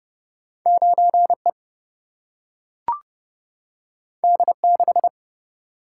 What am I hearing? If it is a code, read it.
Morse code, heard as a keyed tone, about 31 wpm, 700 Hz. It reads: 9E D6
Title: Single-Letter Single-Number 30wpm